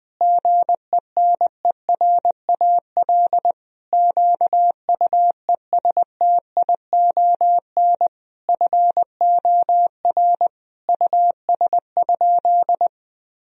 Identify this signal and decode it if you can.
Morse code, heard as a keyed tone, about 20 wpm, 700 Hz. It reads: GENERAL QUESTION FOR US?
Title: Sentences from Top 1000 Words - Review Part 3 - 20wpm